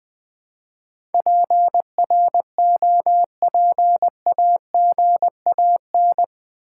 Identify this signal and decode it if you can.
Morse code, heard as a keyed tone, about 20 wpm, 700 Hz. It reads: PROPAGAN
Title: Build A Word - 4 Letters At A Time - 10-Letter Words - 20wpm